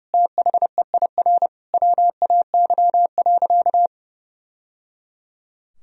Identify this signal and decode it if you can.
Morse code, heard as a keyed tone, about 30 wpm, 700 Hz. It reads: THEIR WAY.
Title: Sentences from Top 1000 Words - Review Part 4 - 30wpm